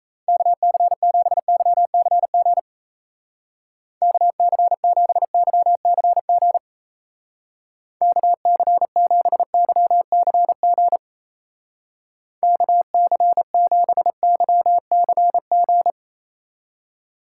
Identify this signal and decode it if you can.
Morse code, heard as a keyed tone, about 34 wpm, 700 Hz. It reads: KC7YCG KC7YCG KC7YCG KC7YCG
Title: Speed Racing - USA 2x3 Call Signs 28wpm